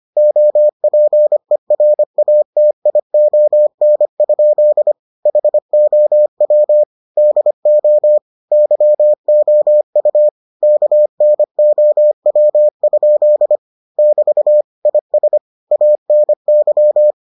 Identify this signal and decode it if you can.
Morse code, heard as a keyed tone, about 25 wpm, 600 Hz. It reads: OPERATION? HOW DO YOU KNOW? <BT> IS ANY